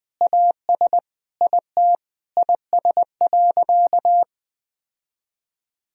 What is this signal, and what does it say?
Morse code, heard as a keyed tone, about 21 wpm, 700 Hz. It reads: AS IT IS.
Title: Sentences from Top 1000 Words - Review Part 3 - 20wpm